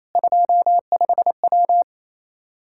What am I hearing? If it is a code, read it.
Morse code, heard as a keyed tone, about 28 wpm, 700 Hz. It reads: 25W
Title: Practice QSO - Exchange 2 - Info - 28wpm